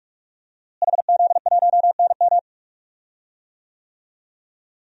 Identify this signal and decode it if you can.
Morse code, heard as a keyed tone, about 46 wpm, 700 Hz. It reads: HZ1NM
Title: DXCC-100 Call Signs - 5 Characters - Repeated 2x - 45wpm